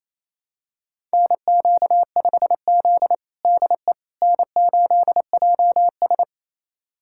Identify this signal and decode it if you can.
Morse code, heard as a keyed tone, about 28 wpm, 700 Hz. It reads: NQ5Z DE N8JS
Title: USA Callsign DE Callsign - 4 Characters - 28wpm